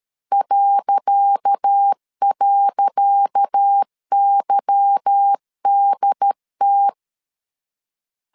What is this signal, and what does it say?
Morse code, heard as a keyed tone, about 13 wpm, 800 Hz. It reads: ..YDT
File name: LEZIONI / 10 SIGNS / 10C DOT 12wpm 2451